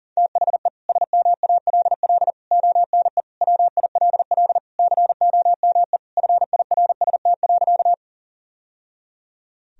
Morse code, heard as a keyed tone, about 37 wpm, 700 Hz. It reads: THE SMALL ONE WILL COME FIRST.
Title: Sentences from Top 1000 Words - Review Part 1 - 40wpm